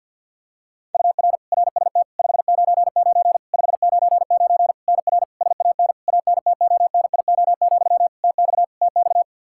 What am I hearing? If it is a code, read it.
Morse code, heard as a keyed tone, about 50 wpm, 700 Hz. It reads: UR RST 599 599 NR SAN ANTONIO, TX TX
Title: K1USN SST QSO Practice 50wpm